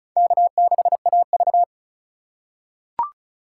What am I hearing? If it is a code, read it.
Morse code, heard as a keyed tone, about 36 wpm, 690 Hz. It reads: K6AV
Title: USA Call Signs - 4 Characters - 2x Character Spacing - 35wpm